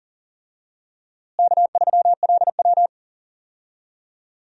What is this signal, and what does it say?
Morse code, heard as a keyed tone, about 40 wpm, 700 Hz. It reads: K3LW
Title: USA Call Signs - 4 Characters - Repeated 3x - 40wpm